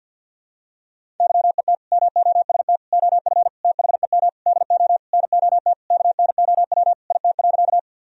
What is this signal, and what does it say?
Morse code, heard as a keyed tone, about 50 wpm, 700 Hz. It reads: YET MOST OF THEM DO NOT KNOW IT.
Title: Sentences from Top 1000 Words - Review Part 1 - 50wpm